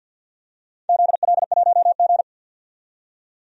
Morse code, heard as a keyed tone, about 53 wpm, 700 Hz. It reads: ZL1G
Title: DXCC-100 Call Signs - 4 Characters - 50wpm